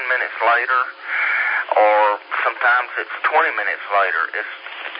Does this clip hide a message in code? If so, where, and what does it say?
Yes: Emergency Alert System (SAME) tones from 0.9 to 4.6 s, heard as harsh buzzing bursts, with speech.